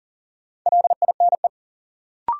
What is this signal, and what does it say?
Morse code, heard as a keyed tone, about 40 wpm, 700 Hz. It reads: LINE
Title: Sentences from Top 1000 Words - Review Part 1 - New Word Drill - 40wpm